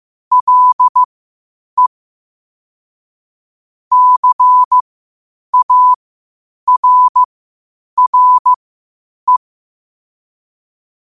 Morse code, heard as a keyed tone, about 15 wpm, spaced as for 8 wpm, 1000 Hz. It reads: LE CARRE
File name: qotdfiles / 8wpm